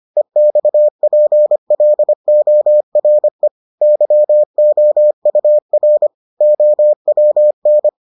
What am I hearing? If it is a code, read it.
Morse code, heard as a keyed tone, about 25 wpm, 600 Hz. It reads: EXPLORE YOUR OWN